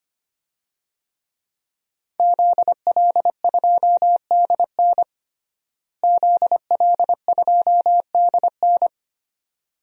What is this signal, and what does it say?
Morse code, heard as a keyed tone, about 25 wpm, 700 Hz. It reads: ZL2DN ZL2DN
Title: DXCC-100 Call Signs - 5 Characters - Repeated 2x - 25wpm